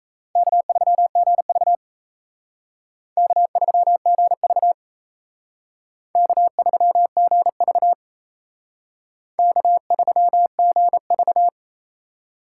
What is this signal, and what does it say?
Morse code, heard as a keyed tone, about 35 wpm, 700 Hz. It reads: K3GV K3GV K3GV K3GV
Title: Speed Racing - USA 1x2 Call Signs 28wpm